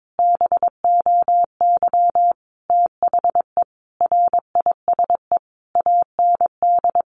Morse code, heard as a keyed tone, about 22 wpm, 700 Hz. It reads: BOY THE RISE AND
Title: Sentences from Top 1500 Words 22wpm